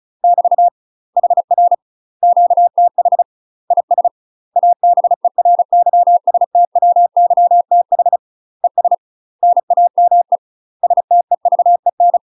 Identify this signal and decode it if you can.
Morse code, heard as a keyed tone, about 36 wpm, 700 Hz. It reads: <BT> HR QTH IS ABERYSTWYTH ES NAME STEVEN
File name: UK MASTER QSOs TRAINING / 35WPM / UK MASTER QSOs 0032